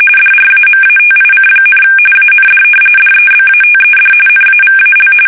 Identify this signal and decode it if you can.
RTTY, about 45.45 baud. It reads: DRAMATIC, AND DEBATED NAVAL
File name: fsk850 / battleshipnj